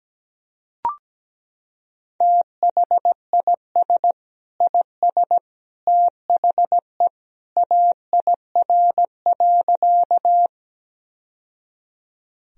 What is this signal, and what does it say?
Morse code, heard as a keyed tone, about 17 wpm, 700 Hz. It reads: THIS IS THE AIR.
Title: Sentences from Top 1000 Words - Review Part 1 - 17wpm